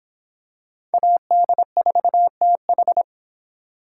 Morse code, heard as a keyed tone, about 25 wpm, 700 Hz. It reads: AD4TH